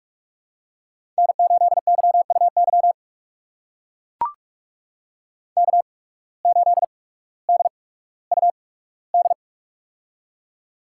Morse code, heard as a keyed tone, about 44 wpm, spaced as for 19 wpm, 700 Hz. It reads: N8YUY K8DUD
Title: USA Call Signs - 5 Characters - 8x Character Spacing - 45wpm